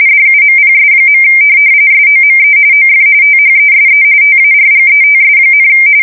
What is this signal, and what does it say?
RTTY, about 45.45 baud. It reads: DUMPS. SOME OF THESE TARGETS,
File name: fsk170 / USS-New-Jersey